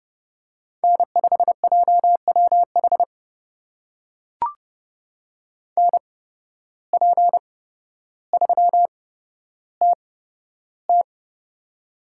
Morse code, heard as a keyed tone, about 29 wpm, spaced as for 13 wpm, 700 Hz. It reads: N5JWH NP3TT
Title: USA Call Signs - 5 Characters - 8x Character Spacing - 30wpm